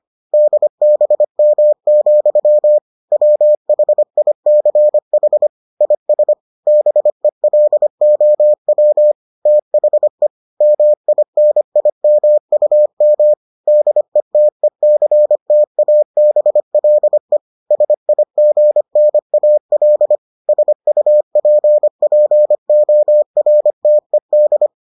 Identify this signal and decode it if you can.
Morse code, heard as a keyed tone, about 25 wpm, 600 Hz. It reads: DBM, WHICH IS BELOW THE MINIMUM DETECTABLE SIGNAL SUPPORTED